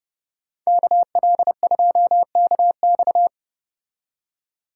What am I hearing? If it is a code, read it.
Morse code, heard as a keyed tone, about 33 wpm, 700 Hz. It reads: KL2KX